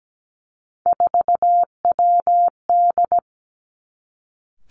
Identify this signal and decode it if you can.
Morse code, heard as a keyed tone, about 17 wpm, 700 Hz. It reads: 4WD
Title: Three Challenging Letter-Number Combinations 17wpm